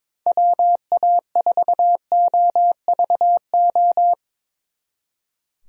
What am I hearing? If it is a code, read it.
Morse code, heard as a keyed tone, about 24 wpm, 700 Hz. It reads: WA4OVO
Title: Speed Racing - USA 2x3 Call Signs 22wpm